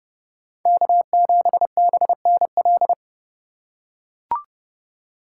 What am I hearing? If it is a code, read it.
Morse code, heard as a keyed tone, about 29 wpm, 700 Hz. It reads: K7BNL